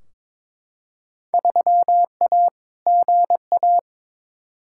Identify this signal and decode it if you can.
Morse code, heard as a keyed tone, about 22 wpm, 700 Hz. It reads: 3A GA
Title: Field Day Exchanges - 22wpm